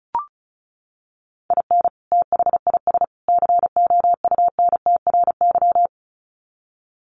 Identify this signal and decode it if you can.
Morse code, heard as a keyed tone, about 34 wpm, 700 Hz. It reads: IN THIS COUNTRY
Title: Sets of 3 Words 35wpm